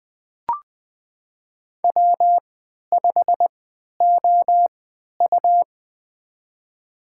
Morse code, heard as a keyed tone, about 20 wpm, spaced as for 11 wpm, 700 Hz. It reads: W5OU
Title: USA Call Signs - 4 Characters - 3x Character Spacing - 20wpm